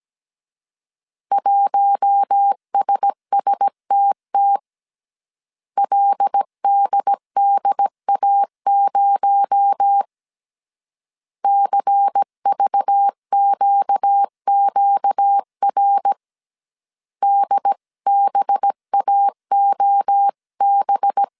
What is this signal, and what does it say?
Morse code, heard as a keyed tone, about 17 wpm, 800 Hz. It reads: 1SSTT LDDA0 CVQQR DBAOB
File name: CW / DET / 1 6 1500C S A 16wpm 2635